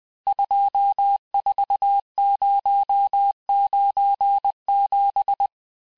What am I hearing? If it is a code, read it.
Morse code, heard as a keyed tone, about 20 wpm, 800 Hz. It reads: 24097
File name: cours-858-texte-chiffre-vitesse-19mpm